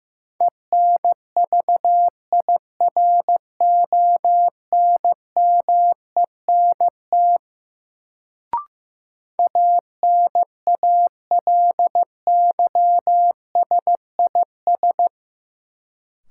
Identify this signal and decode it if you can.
Morse code, heard as a keyed tone, about 15 wpm, 700 Hz. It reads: ENVIRONMENT ANALYSIS
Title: Sentences from Top 1000 Words - Review Part 5 - New Word Drill - 15wpm